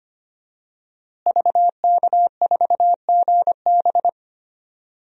Morse code, heard as a keyed tone, about 25 wpm, 700 Hz. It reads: VK4GB
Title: DXCC-100 Call Signs - 5 Characters - Repeated 2x - 25wpm